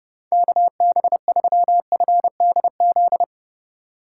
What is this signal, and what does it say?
Morse code, heard as a keyed tone, about 30 wpm, 700 Hz. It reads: KB3FDZ
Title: USA Call Signs - 6 Characters - 6x Character Spacing - 30wpm